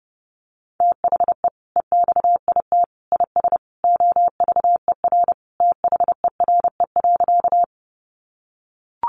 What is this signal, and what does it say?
Morse code, heard as a keyed tone, about 30 wpm, 700 Hz. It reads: THE EXIT IS OVER THERE.
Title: Sentences from Top 1100 Words 30wpm